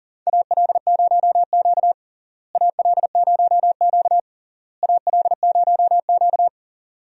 Morse code, heard as a keyed tone, about 40 wpm, 700 Hz. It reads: AL0Q AL0Q AL0Q
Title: USA Call Signs - 4 Characters - Repeated 3x - 40wpm